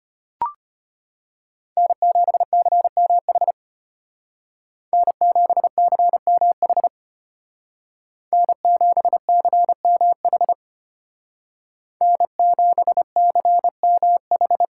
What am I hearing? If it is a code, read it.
Morse code, heard as a keyed tone, about 32 wpm, 700 Hz. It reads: N7CMH N7CMH N7CMH N7CMH